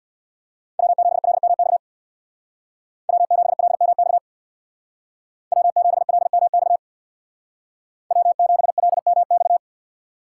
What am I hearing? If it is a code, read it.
Morse code, heard as a keyed tone, about 64 wpm, 700 Hz. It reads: W7LMX W7LMX W7LMX W7LMX